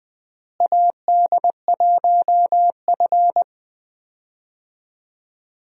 Morse code, heard as a keyed tone, about 20 wpm, 700 Hz. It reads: AD1F
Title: USA Call Signs - 4 Characters - Repeated 2x - 20wpm